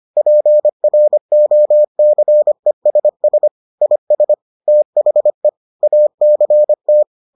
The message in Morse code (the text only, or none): PROCESS IS THE ACT